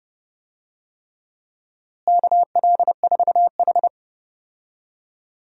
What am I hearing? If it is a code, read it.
Morse code, heard as a keyed tone, about 30 wpm, 700 Hz. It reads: KL4H
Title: USA Call Signs - 4 Characters - Repeated 3x - 30wpm